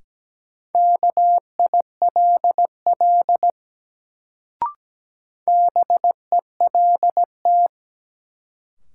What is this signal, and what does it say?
Morse code, heard as a keyed tone, about 17 wpm, 700 Hz. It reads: KILL BELT
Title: Sentences from Top 1400 Words - New Word Drill Alt - 17wpm